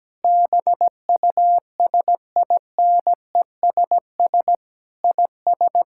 Morse code, heard as a keyed tone, about 17 wpm, 700 Hz. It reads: BUSINESS IS
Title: Sentences from Top 1000 Words - Review Part 5 - 17wpm